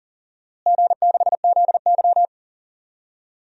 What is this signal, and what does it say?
Morse code, heard as a keyed tone, about 40 wpm, 700 Hz. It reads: G6ZY
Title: DXCC-100 Call Signs - 4 Characters - Repeated 2x - 40wpm